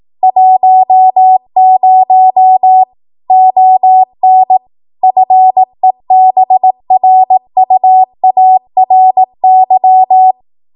Morse code, heard as a keyed tone, about 18 wpm, 750 Hz. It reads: 10 ON FEBRUARY